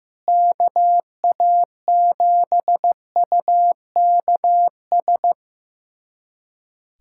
Morse code, heard as a keyed tone, about 15 wpm, 700 Hz. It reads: KA7UKS
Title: USA Callsign DE Callsign - 6 Characters - 15wpm